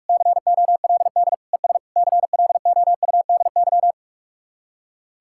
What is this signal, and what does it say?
Morse code, heard as a keyed tone, about 59 wpm, 700 Hz. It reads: COLD ES CLOUDY